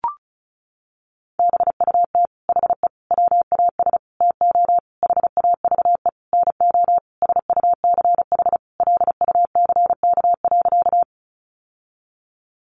Morse code, heard as a keyed tone, about 34 wpm, 700 Hz. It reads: BUT HE WAS TO HAVE NO SUCH LUCK.
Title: Sentences from Top 1500 Words 35wpm